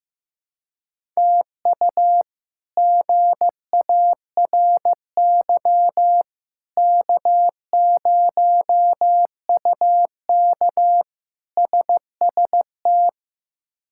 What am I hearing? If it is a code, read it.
Morse code, heard as a keyed tone, about 15 wpm, 700 Hz. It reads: TU GARY K0UK SST
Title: K1USN SST QSO Practice - 15wpm